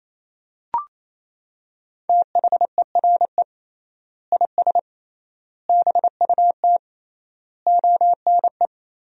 Morse code, heard as a keyed tone, about 28 wpm, 700 Hz. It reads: THERE IS BUT ONE
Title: 3x Word-Spacing - Sentences from Top 100 Words - 28wpm